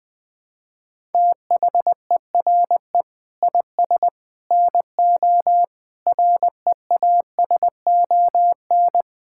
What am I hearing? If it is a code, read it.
Morse code, heard as a keyed tone, about 20 wpm, 700 Hz. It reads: THERE IS NO REASON
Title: Sentences with 6-Word Phrases - 20wpm